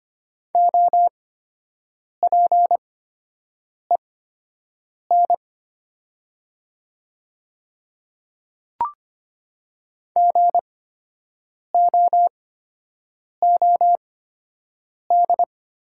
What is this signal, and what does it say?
Morse code, heard as a keyed tone, about 25 wpm, spaced as for 7 wpm, 700 Hz. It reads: OPEN GOOD
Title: 8x Character Spacing - Top 300 Words - 25wpm